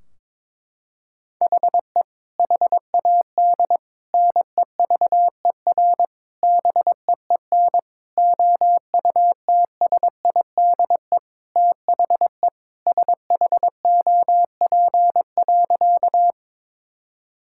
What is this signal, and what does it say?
Morse code, heard as a keyed tone, about 22 wpm, 700 Hz. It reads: HE HAD NEVER BEEN OUTSIDE THE SHOP.
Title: Sentences from Top 1600 Words 22wpm